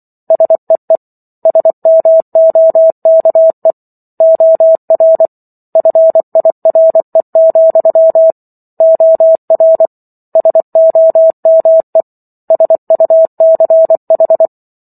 Morse code, heard as a keyed tone, about 24 wpm, 650 Hz. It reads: SEE SMOKE OR FIRE, OR SOME SUCH